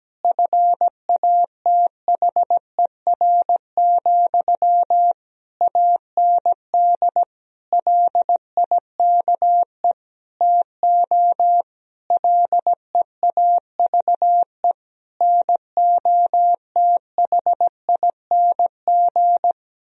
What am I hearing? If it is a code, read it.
Morse code, heard as a keyed tone, about 17 wpm, 700 Hz. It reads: FATHER, AND LIKE TO LEAVE NOTHING
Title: Sentences from Top 1000 Words - Review Part 1 - 17wpm